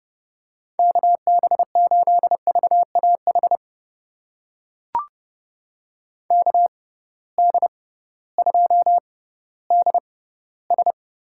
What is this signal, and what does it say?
Morse code, heard as a keyed tone, about 30 wpm, spaced as for 16 wpm, 700 Hz. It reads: KB8VAH KD2DS
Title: USA Call Signs - 6 Characters - 6x Character Spacing - 30wpm